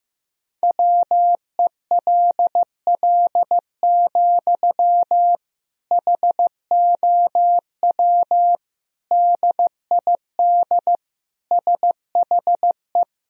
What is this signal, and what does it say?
Morse code, heard as a keyed tone, about 15 wpm, 700 Hz. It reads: WELL, HOW DID SHE
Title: Sentences from Top 1000 Words - Review Part 2 - 15wpm